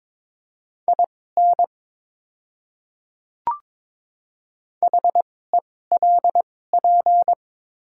Morse code, heard as a keyed tone, about 23 wpm, spaced as for 16 wpm, 700 Hz. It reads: IN HELP
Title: 2x Character Spacing - Top 300 Words - 22wpm